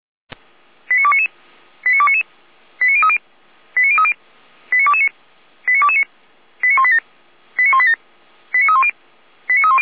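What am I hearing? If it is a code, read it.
Five-tone selective call (ZVEI), heard as quick runs of five tones, tones of about 70 ms. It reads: 89299 89299 89030 89030 89229 89229 89187 89187 89310 89310
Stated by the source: ZVEI five-tone call 89174 received by SDR on channel 455.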